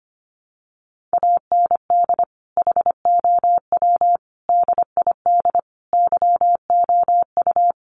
Morse code, heard as a keyed tone, about 25 wpm, 700 Hz. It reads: AND HOW DID YOU
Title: Sentences from Top 1400 Words 25wpm